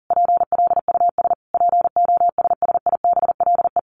Morse code, heard as a keyed tone, about 40 wpm, 700 Hz. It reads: PLUS POSSIBLE
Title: A Sound Of Thunder 40wpm